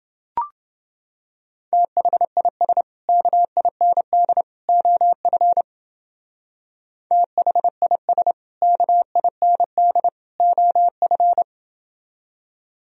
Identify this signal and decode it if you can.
Morse code, heard as a keyed tone, about 29 wpm, 700 Hz. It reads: THIS KIND OF THIS KIND OF